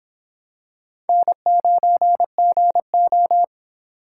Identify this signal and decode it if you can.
Morse code, heard as a keyed tone, about 26 wpm, 700 Hz. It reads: N9GO